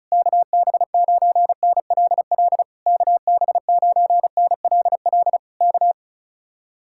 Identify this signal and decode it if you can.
Morse code, heard as a keyed tone, about 37 wpm, 700 Hz. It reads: KB9NLL KB9NLL K
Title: Speed-Racing Common QSO Phrases Encore - 30wpm